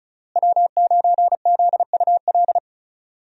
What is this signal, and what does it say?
Morse code, heard as a keyed tone, about 34 wpm, 700 Hz. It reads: W9ZUL
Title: USA Call Signs - 5 Characters - Repeated 3x - 35wpm